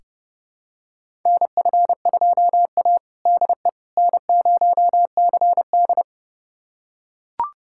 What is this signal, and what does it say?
Morse code, heard as a keyed tone, about 30 wpm, 700 Hz. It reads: NF2A DE N0CD
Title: USA Callsign DE Callsign - 4 Characters - 30wpm